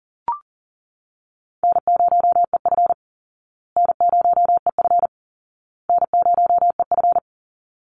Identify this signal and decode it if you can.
Morse code, heard as a keyed tone, about 37 wpm, 700 Hz. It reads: N0EF N0EF N0EF
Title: Call Signs Repeated 2x - 40wpm